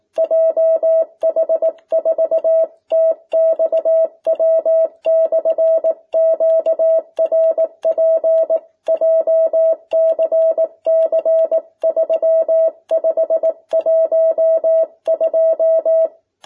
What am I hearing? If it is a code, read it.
Morse code, heard as a keyed tone, about 18 wpm, 650 Hz. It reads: JH4TXW/QRP JCC 3512
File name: jh8cbh / 2021cw